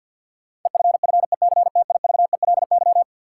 Morse code, heard as a keyed tone, about 53 wpm, 700 Hz. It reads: EFFECTIVELY
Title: Sentences from Top 1400 Words - New Word Drill - 50wpm